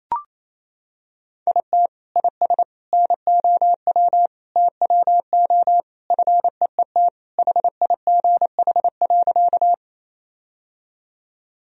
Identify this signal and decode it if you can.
Morse code, heard as a keyed tone, about 28 wpm, 700 Hz. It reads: IT IS NOW TWO FEET HIGH.
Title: Sentences from Top 1000 Words - Review Part 1 - 28wpm